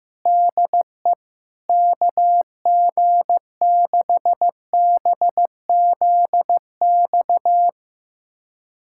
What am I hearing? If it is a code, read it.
Morse code, heard as a keyed tone, about 15 wpm, 700 Hz. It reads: DE KG6BZX
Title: USA Callsign DE Callsign - 6 Characters - 15wpm